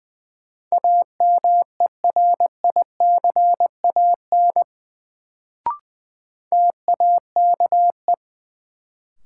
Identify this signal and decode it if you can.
Morse code, heard as a keyed tone, about 21 wpm, 700 Hz. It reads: AMERICAN TAKE
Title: Sets of 2 Words - New Word Drill - 20wpm